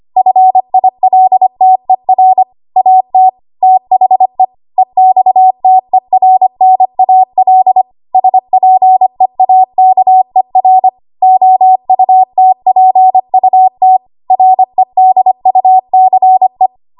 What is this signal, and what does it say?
Morse code, heard as a keyed tone, about 25 wpm, 750 Hz. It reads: FILTER AT THE EXTERNAL SPEAKER OUTPUT REDUCE
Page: W1AW Web Code Practice Files